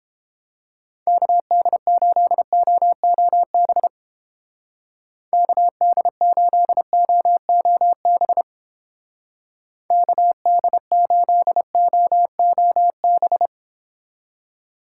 Morse code, heard as a keyed tone, about 31 wpm, 700 Hz. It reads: KD8OOB KD8OOB KD8OOB